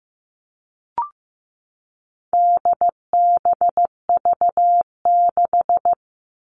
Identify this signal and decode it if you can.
Morse code, heard as a keyed tone, about 15 wpm, 700 Hz. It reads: DBV6
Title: All Cut Numbers - 4-Digit 15wpm